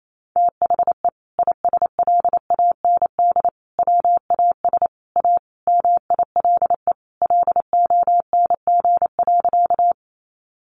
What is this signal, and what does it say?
Morse code, heard as a keyed tone, about 28 wpm, 700 Hz. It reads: THE ISLAND WAS A MILE LONG.
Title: Sentences from Top 500 Words 28wpm